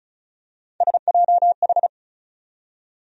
Morse code, heard as a keyed tone, about 36 wpm, 700 Hz. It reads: SJH
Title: USA Call Sign Postfixes - 3 Letters 35wpm